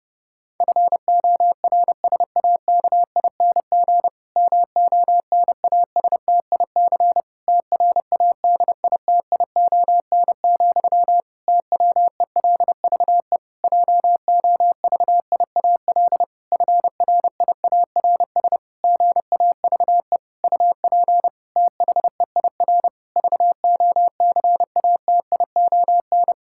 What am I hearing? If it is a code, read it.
Morse code, heard as a keyed tone, about 30 wpm, 700 Hz. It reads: FORSAKING MONASTIC TRADITION, TWELVE JOVIAL FRIARS GAVE UP THEIR VOCATION
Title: Pangram Challenge v2 - 30wpm